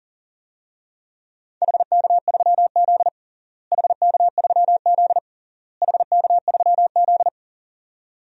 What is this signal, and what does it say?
Morse code, heard as a keyed tone, about 40 wpm, 700 Hz. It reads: HK3Z HK3Z HK3Z
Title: DXCC-100 Call Signs - 4 Characters - Repeated 3x - 40wpm